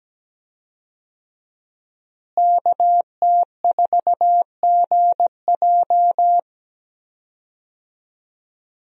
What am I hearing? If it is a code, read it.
Morse code, heard as a keyed tone, about 17 wpm, 700 Hz. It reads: KT4GJ
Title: USA Call Signs - 5 Characters - Repeated 2x - 17wpm